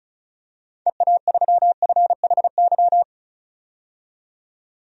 Morse code, heard as a keyed tone, about 36 wpm, 700 Hz. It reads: EA3FHY
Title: DXCC-100 Call Signs - 6 Characters - 35wpm